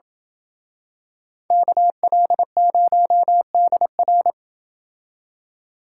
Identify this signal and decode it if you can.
Morse code, heard as a keyed tone, about 27 wpm, 700 Hz. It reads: KL0DR